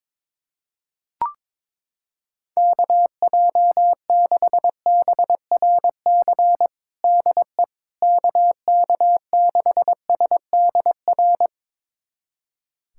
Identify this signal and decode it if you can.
Morse code, heard as a keyed tone, about 22 wpm, 700 Hz. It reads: KJ6BRC DE KK6SDR
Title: USA Callsign DE Callsign - 6 Characters - 22wpm